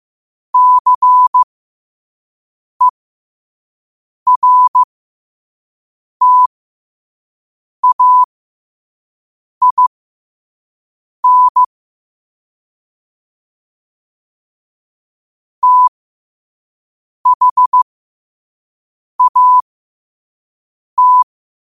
Morse code, heard as a keyed tone, about 15 wpm, spaced as for 5 wpm, 1000 Hz. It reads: CERTAIN THAT
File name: qotdfiles / 5wpm